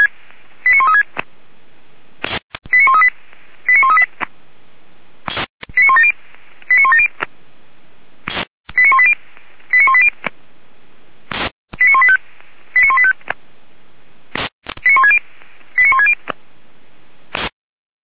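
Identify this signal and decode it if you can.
Five-tone selective call (ZVEI), heard as quick runs of five tones, tones of about 70 ms. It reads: 89136 89138 89138 89170 89170 89180 89180 89185 89185 89160 89160
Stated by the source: ZVEI five-tone call received by SDR, fire brigade Orlamünde (feuerwehr-orlamuende).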